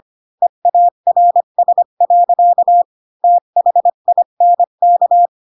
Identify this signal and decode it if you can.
Morse code, heard as a keyed tone, about 25 wpm, 700 Hz. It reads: EARS. THINK
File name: Pinocchio / Pinocchio ch3 20 25 ik1hge